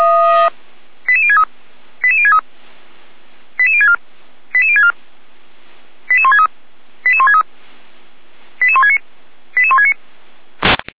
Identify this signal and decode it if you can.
Five-tone selective call (ZVEI), heard as quick runs of five tones, tones of about 70 ms. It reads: 88063 88063 88064 88064 88163 88163 88169 88169
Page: SDR/ZVEI